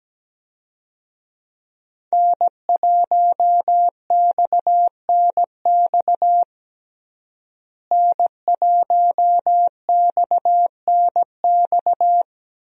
Morse code, heard as a keyed tone, about 17 wpm, 700 Hz. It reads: N1XNX N1XNX
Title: USA Call Signs - 5 Characters - Repeated 3x - 17wpm